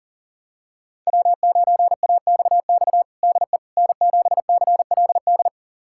Morse code, heard as a keyed tone, about 40 wpm, 700 Hz. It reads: W9AXX DE N7CLD
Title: USA Callsign DE Callsign - 5 Characters - 40wpm